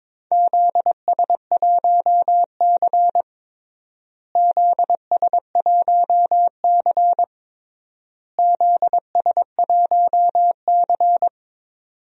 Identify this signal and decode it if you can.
Morse code, heard as a keyed tone, about 22 wpm, 700 Hz. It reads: ZS1C ZS1C ZS1C
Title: DXCC-100 Call Signs - 4 Characters - Repeated 3x - 22wpm